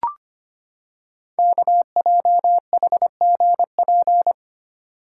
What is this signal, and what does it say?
Morse code, heard as a keyed tone, about 25 wpm, 700 Hz. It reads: KJHGP
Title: Code Groups - 5 Letters - 25wpm